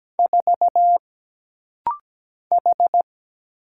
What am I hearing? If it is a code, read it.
Morse code, heard as a keyed tone, about 17 wpm, 700 Hz. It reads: 4 H
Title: Single Letter-Number Alt - Rapid-Fire 17wpm